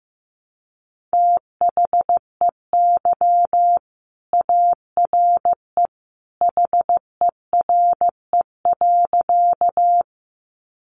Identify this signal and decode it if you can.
Morse code, heard as a keyed tone, about 15 wpm, 700 Hz. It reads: THEY ARE HERE.
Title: Sentences from Top 200 Words 15wpm